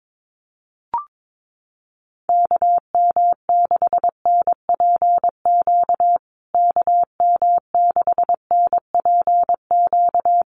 Morse code, heard as a keyed tone, about 22 wpm, 700 Hz. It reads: KM6NPQ KM6NPQ
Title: USA Call Signs Repeated 3x 22wpm